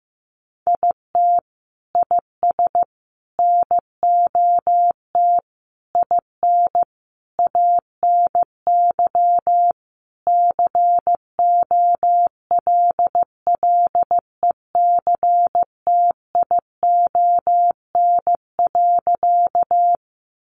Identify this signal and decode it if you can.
Morse code, heard as a keyed tone, about 15 wpm, 700 Hz. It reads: IT IS NOT IN ANY COLLECTION.
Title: Sentences from Top 1200 Words 15wpm